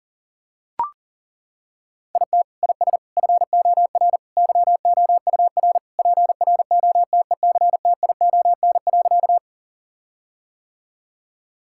Morse code, heard as a keyed tone, about 37 wpm, 700 Hz. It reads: IT IS FOR YOUR PROTECTION.
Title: Sentences from Top 1600 Words 40wpm